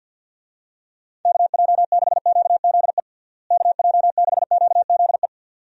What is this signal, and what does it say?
Morse code, heard as a keyed tone, about 51 wpm, 700 Hz. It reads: KJ6QZE KJ6QZE
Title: USA Call Signs - 6 Characters - Repeated 2x - 50wpm